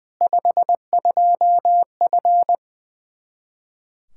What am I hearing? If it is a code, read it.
Morse code, heard as a keyed tone, about 20 wpm, 700 Hz. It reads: 52F
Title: Practice QSO - Exchange 2 - Info - 20wpm